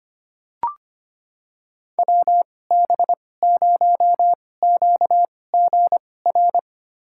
Morse code, heard as a keyed tone, about 25 wpm, spaced as for 18 wpm, 700 Hz. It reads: WB0QGR
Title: USA Call Signs - 6 Characters - 2x Character Spacing - 25wpm